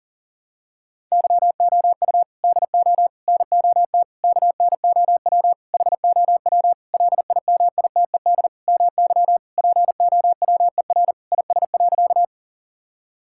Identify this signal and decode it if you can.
Morse code, heard as a keyed tone, about 37 wpm, 700 Hz. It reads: YOU DO NOT KNOW HOW LIMITED MY POWER IS.
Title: Sentences from Top 1000 Words - Review Part 6 - 40wpm